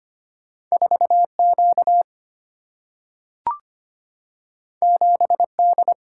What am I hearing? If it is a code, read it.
Morse code, heard as a keyed tone, about 25 wpm, 700 Hz. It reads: 4Q 7D
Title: Single-Letter Single-Number 25wpm